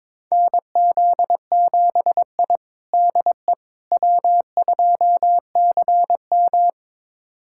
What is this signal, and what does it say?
Morse code, heard as a keyed tone, about 22 wpm, 700 Hz. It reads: NZ7I DE W2CM
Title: USA Callsign DE Callsign - 4 Characters - 22wpm